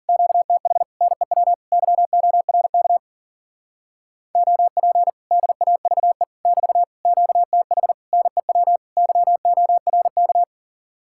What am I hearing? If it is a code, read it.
Morse code, heard as a keyed tone, about 52 wpm, 700 Hz. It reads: QTH NEW YORK OP DAVE <BT> QTH NEW YORK